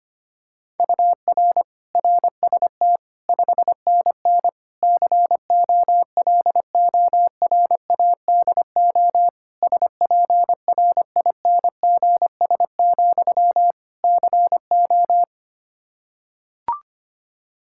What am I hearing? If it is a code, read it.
Morse code, heard as a keyed tone, about 25 wpm, 700 Hz. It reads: UR RST 5NN COLORADO SPRINGS, CO
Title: Practice QSO - Exchange 1 - Beginning - 25wpm